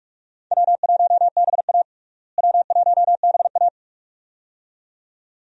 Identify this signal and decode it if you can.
Morse code, heard as a keyed tone, about 41 wpm, 700 Hz. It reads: W1BA W1BA
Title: USA Call Signs - 4 Characters - Repeated 2x - 45wpm